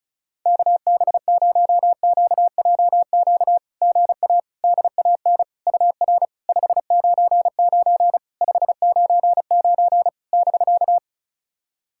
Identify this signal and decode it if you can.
Morse code, heard as a keyed tone, about 36 wpm, 700 Hz. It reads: KB0QJQ GA DAN UR 599 599 <BK>
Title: SOTA QSO Practice 35wpm